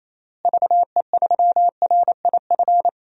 Morse code, heard as a keyed tone, about 28 wpm, 700 Hz. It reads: VE3RIF
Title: DXCC-100 Call Signs - 6 Characters - Repeated 3x - 28wpm